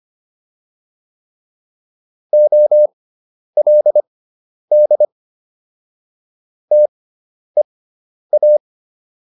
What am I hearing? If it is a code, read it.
Morse code, heard as a keyed tone, about 25 wpm, spaced as for 10 wpm, 600 Hz. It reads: OLD TEA